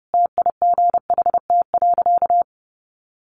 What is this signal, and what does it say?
Morse code, heard as a keyed tone, about 30 wpm, 700 Hz. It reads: TIGHT.
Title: A Sound Of Thunder 30wpm